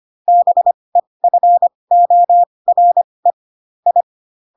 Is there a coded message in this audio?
Yes: Morse code from 0.3 to 4.0 s, heard as a keyed tone, about 25 wpm, spaced as for 20 wpm, 700 Hz. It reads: BEFORE I